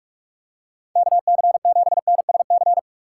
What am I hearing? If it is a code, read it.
Morse code, heard as a keyed tone, about 43 wpm, 700 Hz. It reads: KC7NSC